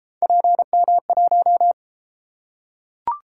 Morse code, heard as a keyed tone, about 31 wpm, 700 Hz. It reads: PM1